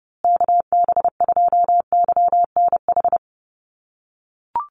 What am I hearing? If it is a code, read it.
Morse code, heard as a keyed tone, about 30 wpm, 700 Hz. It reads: KB2YNH
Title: USA Call Signs Repeated 3x 30wpm